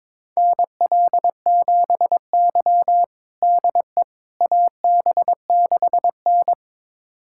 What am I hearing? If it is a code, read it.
Morse code, heard as a keyed tone, about 22 wpm, 700 Hz. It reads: NL7Y DE AB6N
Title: USA Callsign DE Callsign - 4 Characters - 22wpm